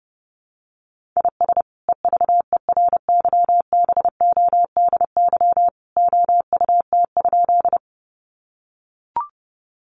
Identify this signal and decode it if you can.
Morse code, heard as a keyed tone, about 30 wpm, 700 Hz. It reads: IS EVERYBODY OUT?
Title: Sentences from Top 1400 Words 30wpm